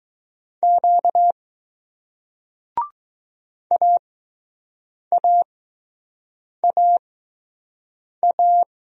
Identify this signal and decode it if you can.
Morse code, heard as a keyed tone, about 20 wpm, spaced as for 6 wpm, 700 Hz. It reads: Q AAAA